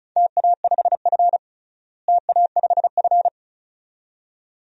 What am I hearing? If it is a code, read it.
Morse code, heard as a keyed tone, about 36 wpm, 700 Hz. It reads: TA5F TA5F
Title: DXCC-100 Call Signs - 4 Characters - Repeated 2x - 35wpm